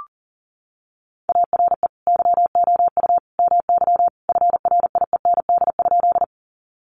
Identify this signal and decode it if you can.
Morse code, heard as a keyed tone, about 40 wpm, 700 Hz. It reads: ARE YOU MY FRIEND?
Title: Sentences from Top 300 Words 40wpm